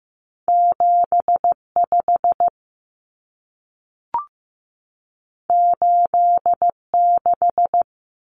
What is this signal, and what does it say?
Morse code, heard as a keyed tone, about 15 wpm, 700 Hz. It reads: 7 5 8 6
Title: Numbers - 2-Digit 15wpm